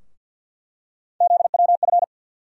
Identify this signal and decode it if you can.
Morse code, heard as a keyed tone, about 51 wpm, 700 Hz. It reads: ZWF
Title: USA Call Sign Postfixes - 3 Letters 50wpm